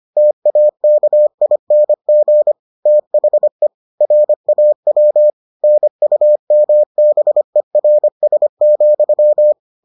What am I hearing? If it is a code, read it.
Morse code, heard as a keyed tone, about 25 wpm, 600 Hz. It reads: TAKING THE RAW NUMBERS,